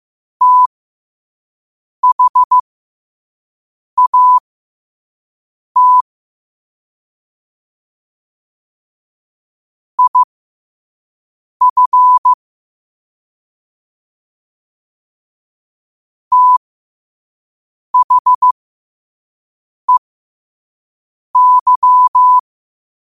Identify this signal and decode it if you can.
Morse code, heard as a keyed tone, about 15 wpm, spaced as for 5 wpm, 1000 Hz. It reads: THAT IF THEY
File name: qotdfiles / 5wpm